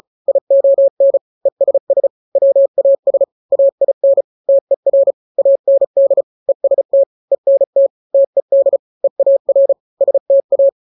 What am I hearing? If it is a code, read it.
Morse code, heard as a keyed tone, about 34 wpm, 550 Hz. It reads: ION ESS WAS AIN TER AND EST ENT TED EAR STA
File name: 3-grams 35 30 WPM